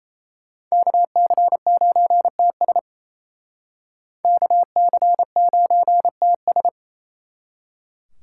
Morse code, heard as a keyed tone, about 35 wpm, 700 Hz. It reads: KC9TS KC9TS